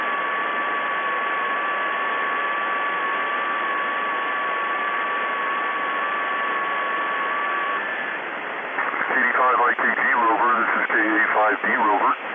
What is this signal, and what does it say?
FT8, with speech.